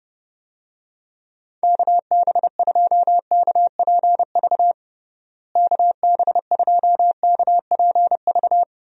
Morse code, heard as a keyed tone, about 30 wpm, 700 Hz. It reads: KB2KPV KB2KPV
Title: USA Call Signs - 6 Characters - Repeated 3x - 30wpm